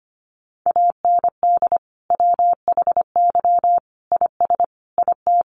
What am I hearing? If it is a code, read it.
Morse code, heard as a keyed tone, about 25 wpm, 700 Hz. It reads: AND WHY IS IT
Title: Sentences from Top 1100 Words 25wpm